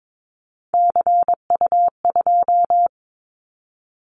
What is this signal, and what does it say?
Morse code, heard as a keyed tone, about 22 wpm, 700 Hz. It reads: CU2
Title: Callsign Prefixes for Top 50 DXCC - 22wpm